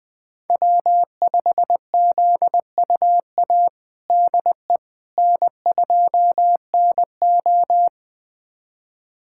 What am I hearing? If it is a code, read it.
Morse code, heard as a keyed tone, about 21 wpm, 700 Hz. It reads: W5ZUA DE N2NO
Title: USA Callsign DE Callsign 20wpm